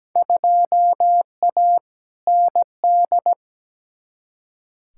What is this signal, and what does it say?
Morse code, heard as a keyed tone, about 17 wpm, 700 Hz. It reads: 2A ND
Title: Field Day Exchanges Encore - 17wpm